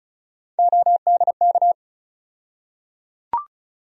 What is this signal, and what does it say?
Morse code, heard as a keyed tone, about 36 wpm, 700 Hz. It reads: ODK
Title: USA Call Sign Postfixes - 3 Letters 35wpm